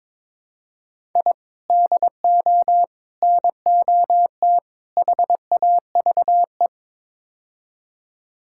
Morse code, heard as a keyed tone, about 22 wpm, 700 Hz. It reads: I DO NOT HAVE
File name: practice / 22wpm / Build A Phrase - 5-Word Phrases - 22wpm